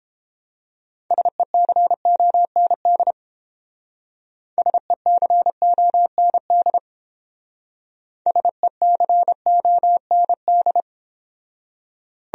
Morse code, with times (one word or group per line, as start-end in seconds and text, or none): 1.1-3.1 SECOND
4.6-6.8 SECOND
8.3-10.8 SECOND